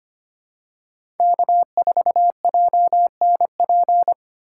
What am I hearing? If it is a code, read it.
Morse code, heard as a keyed tone, about 25 wpm, 700 Hz. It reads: K4JNP
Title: USA Call Signs - 5 Characters - 3x Character Spacing - 25wpm